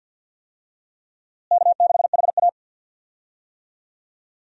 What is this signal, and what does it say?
Morse code, heard as a keyed tone, about 52 wpm, 700 Hz. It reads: K6HA
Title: USA Call Signs - 4 Characters - Repeated 3x - 50wpm